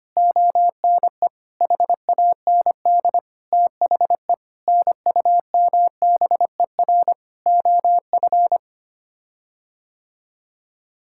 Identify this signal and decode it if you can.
Morse code, heard as a keyed tone, about 25 wpm, 700 Hz. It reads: ONE HAND THE NUMBER OF
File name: Sets of 9 Words - 25wpm